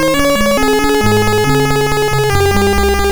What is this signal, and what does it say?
There is no code.